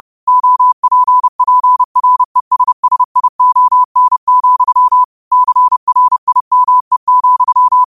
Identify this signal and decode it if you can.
Morse code, heard as a keyed tone, about 30 wpm, 1000 Hz. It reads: OPPRESSION, CRIME,
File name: qotdfiles / 30wpm